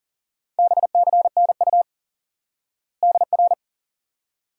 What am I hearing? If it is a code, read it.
Morse code, heard as a keyed tone, about 37 wpm, 700 Hz. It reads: BCNU DR
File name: IQE Course 3 - Lesson 89 - YAGI - Two Elements - 40wpm